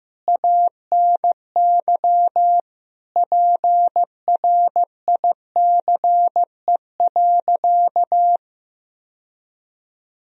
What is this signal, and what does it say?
Morse code, heard as a keyed tone, about 16 wpm, 700 Hz. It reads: ANY PRICE.
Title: Sentences from Top 1000 Words - Review Part 4 - 15wpm